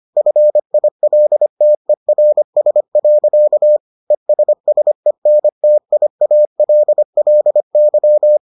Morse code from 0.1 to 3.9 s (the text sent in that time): FILTERS.